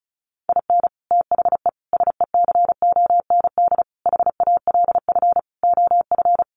Morse code, heard as a keyed tone, about 34 wpm, 700 Hz. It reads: IN THE SECOND HALF OF
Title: Sets of 5 Words 35wpm